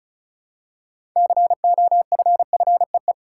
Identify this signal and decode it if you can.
Morse code, heard as a keyed tone, about 34 wpm, 700 Hz. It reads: COFFEE
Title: Six Letter Words - 35wpm